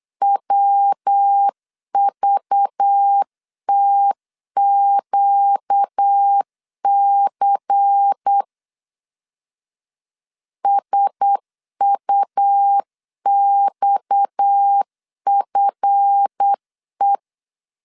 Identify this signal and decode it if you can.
Morse code, heard as a keyed tone, about 8 wpm, 800 Hz. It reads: WVTQC SUXFE
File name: LEZIONI / 7 GF9Z / 7 GF 8wpm 3108